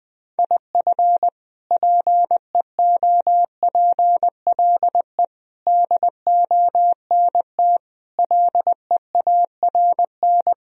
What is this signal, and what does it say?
Morse code, heard as a keyed tone, about 20 wpm, 700 Hz. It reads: IF PEOPLE DONT LEARN
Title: Sentences from Top 1000 Words - Review Part 3 - 20wpm